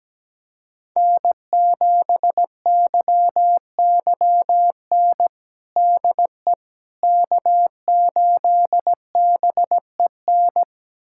Morse code, heard as a keyed tone, about 17 wpm, 700 Hz. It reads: N7YYN DE K8BEN
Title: USA Callsign DE Callsign - 5 Characters - 17wpm